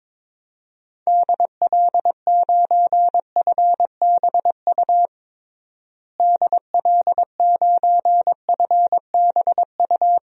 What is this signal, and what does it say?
Morse code, heard as a keyed tone, about 22 wpm, 700 Hz. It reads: DL9FBU DL9FBU
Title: DXCC-100 Call Signs - 6 Characters - Repeated 3x - 22wpm